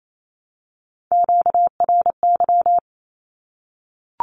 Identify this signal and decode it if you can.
Morse code, heard as a keyed tone, about 28 wpm, 700 Hz. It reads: QRY
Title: Q-Codes 28wpm